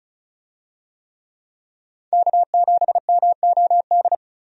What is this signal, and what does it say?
Morse code, heard as a keyed tone, about 36 wpm, 700 Hz. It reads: K7MOD
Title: USA Call Signs - 5 Characters - 35wpm